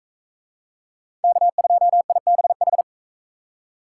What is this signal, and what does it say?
Morse code, heard as a keyed tone, about 42 wpm, 700 Hz. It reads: K2IBH